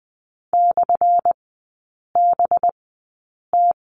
Morse code, heard as a keyed tone, about 20 wpm, spaced as for 8 wpm, 700 Hz. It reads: /BT
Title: 212 Lesson 34e - Number 8 - Three Characters 20wpm